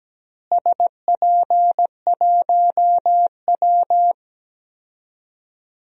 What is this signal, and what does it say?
Morse code, heard as a keyed tone, about 17 wpm, 700 Hz. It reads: SP1W
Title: DXCC-100 Call Signs - 4 Characters - Repeated 3x - 17wpm